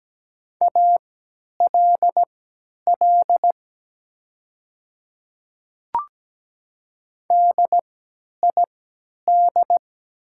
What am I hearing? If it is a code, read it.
Morse code, heard as a keyed tone, about 17 wpm, spaced as for 10 wpm, 700 Hz. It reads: ALL DID
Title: 3x Character Spacing - Top 100 Words - 17wpm